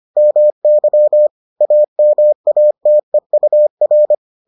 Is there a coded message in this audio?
Yes: Morse code from 0.2 to 4.1 s, heard as a keyed tone, about 25 wpm, 600 Hz. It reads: MY AMATEUR